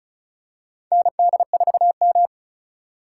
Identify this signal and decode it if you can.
Morse code, heard as a keyed tone, about 36 wpm, 700 Hz. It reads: ND4M
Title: USA Call Signs - 8x Character Spacing - 35wpm